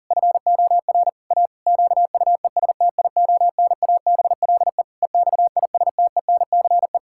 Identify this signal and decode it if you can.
Morse code, heard as a keyed tone, about 40 wpm, 700 Hz. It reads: FOR A QUESTIONABLE EXISTENCE
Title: Pangram Challenge v2 - 40wpm